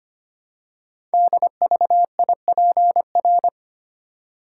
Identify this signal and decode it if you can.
Morse code, heard as a keyed tone, about 25 wpm, 700 Hz. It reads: DVIPR
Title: Code Groups - 5 Letters - 25wpm